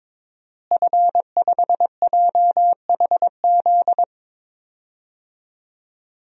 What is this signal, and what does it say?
Morse code, heard as a keyed tone, about 22 wpm, 700 Hz. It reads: F5JHZ
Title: DXCC-100 Call Signs - 5 Characters - Repeated 2x - 22wpm